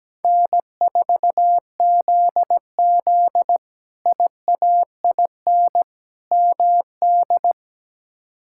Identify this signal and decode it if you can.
Morse code, heard as a keyed tone, about 17 wpm, 700 Hz. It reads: N4ZZ IAIN MD
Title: K1USN SST QSO Practice - 17wpm